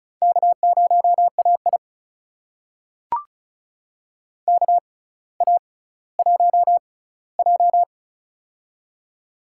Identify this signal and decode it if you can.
Morse code, heard as a keyed tone, about 34 wpm, spaced as for 18 wpm, 700 Hz. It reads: K0AI KA1J
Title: USA Call Signs - 4 Characters - 6x Character Spacing - 35wpm